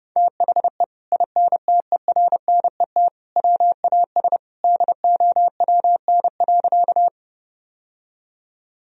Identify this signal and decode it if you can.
Morse code, heard as a keyed tone, about 30 wpm, 700 Hz. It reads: THE INTERNET WAS DOWN.
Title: Sentences from Top 1000 Words - Review Part 4 - 30wpm